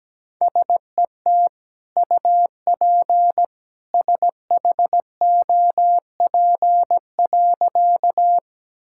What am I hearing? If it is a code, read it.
Morse code, heard as a keyed tone, about 17 wpm, 700 Hz. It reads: SET UP SHOP.
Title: Sentences from Top 1600 Words 17wpm